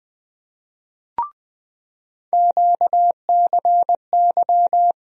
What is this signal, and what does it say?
Morse code, heard as a keyed tone, about 20 wpm, 700 Hz. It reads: QCY
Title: USA Call Sign Postfixes - 3 Letters 20wpm